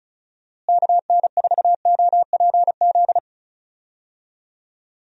Morse code, heard as a keyed tone, about 34 wpm, 700 Hz. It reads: KN4OPZ
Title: USA Call Signs - 6 Characters - Repeated 2x - 35wpm